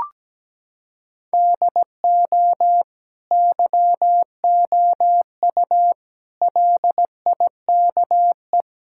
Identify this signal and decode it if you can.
Morse code, heard as a keyed tone, about 17 wpm, 700 Hz. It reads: DO YOU LIKE
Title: Sentences from Top 1000 Words - Review Part 5 - 17wpm